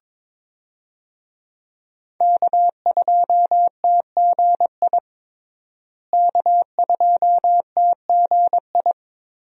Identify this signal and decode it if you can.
Morse code, heard as a keyed tone, about 22 wpm, 700 Hz. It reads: K2TGI K2TGI
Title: USA Call Signs - 5 Characters - Repeated 3x - 22wpm